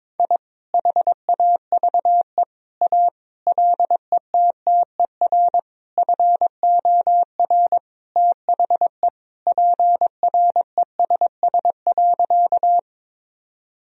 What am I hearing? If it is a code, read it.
Morse code, heard as a keyed tone, about 22 wpm, 700 Hz. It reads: I HAVE A LETTER FOR THE PRESS.
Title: Sentences from Top 1000 Words - Review Part 1 - 22wpm